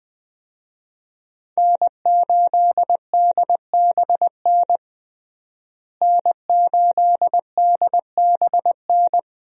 Morse code, heard as a keyed tone, about 20 wpm, 700 Hz. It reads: N8DBN N8DBN
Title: USA Call Signs - 5 Characters - Repeated 2x - 20wpm